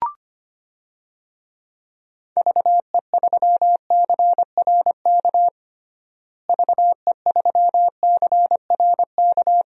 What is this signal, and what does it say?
Morse code, heard as a keyed tone, about 25 wpm, 700 Hz. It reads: VE3CRK VE3CRK
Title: DXCC-100 Call Signs - 6 Characters - Repeated 3x - 25wpm